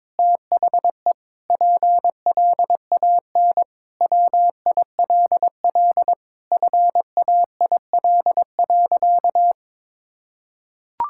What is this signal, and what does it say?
Morse code, heard as a keyed tone, about 22 wpm, 700 Hz. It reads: THE PLAN WILL FAIL.
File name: Sentences with 7-Word Phrases - 22wpm